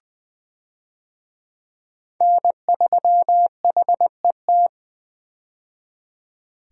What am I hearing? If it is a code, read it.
Morse code, heard as a keyed tone, about 20 wpm, 700 Hz. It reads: N3HET
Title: USA Call Signs - 5 Characters - 20wpm